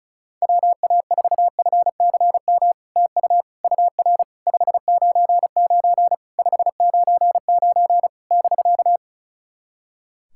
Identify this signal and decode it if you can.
Morse code, heard as a keyed tone, about 36 wpm, 700 Hz. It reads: WA4FCM TU UR 599 599 <BK>
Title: SOTA QSO Practice 35wpm